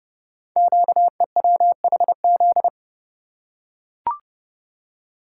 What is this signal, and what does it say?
Morse code, heard as a keyed tone, about 30 wpm, 700 Hz. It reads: QEWHZ
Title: Code Groups - 5 Letter:Numbers - 30wpm